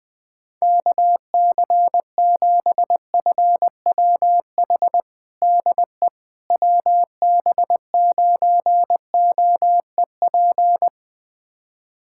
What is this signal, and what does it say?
Morse code, heard as a keyed tone, about 21 wpm, 700 Hz. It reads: KC7FWH DE WB9OEP
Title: USA Callsign DE Callsign - 6 Characters - 20wpm